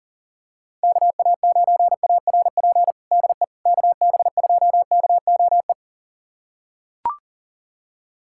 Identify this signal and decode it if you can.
Morse code, heard as a keyed tone, about 40 wpm, 700 Hz. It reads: KA9ARP DE KB2KOE
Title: USA Callsign DE Callsign - 6 Characters - 40wpm